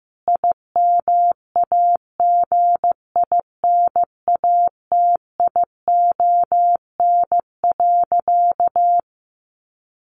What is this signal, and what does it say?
Morse code, heard as a keyed tone, about 15 wpm, 700 Hz. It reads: IMAGINATION.
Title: Sentences from Top 1300 Words 15wpm